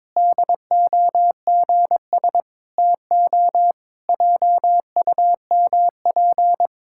Morse code, heard as a keyed tone, about 22 wpm, 700 Hz. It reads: DOGS TO JUMP
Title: Pangram Challenge v2 - 22wpm